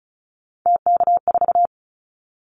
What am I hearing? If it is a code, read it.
Morse code, heard as a keyed tone, about 36 wpm, 700 Hz. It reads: TK4
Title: Callsign Prefixes for Top 100 DXCC - 35wpm